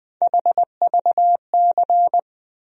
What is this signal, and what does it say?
Morse code, heard as a keyed tone, about 20 wpm, 700 Hz. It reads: HVC
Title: USA Call Sign Postfixes - 3 Letters 20wpm